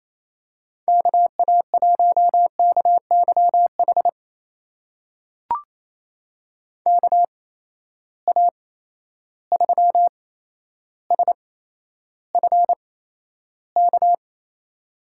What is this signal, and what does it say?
Morse code, heard as a keyed tone, about 28 wpm, spaced as for 12 wpm, 700 Hz. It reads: KA1KYH KA3SFK
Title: USA Call Signs - 8x Character Spacing - 28wpm